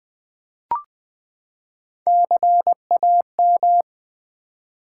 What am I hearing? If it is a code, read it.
Morse code, heard as a keyed tone, about 20 wpm, 700 Hz. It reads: CAM
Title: Build A Word - 3 Letters At A Time - 8-Letter Words - 20wpm